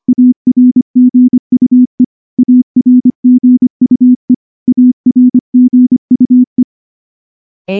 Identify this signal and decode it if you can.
Morse code, heard as a keyed tone, about 24 wpm, 250 Hz. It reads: ARGUE ARGUE ARGUE
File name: morse-training-25WPM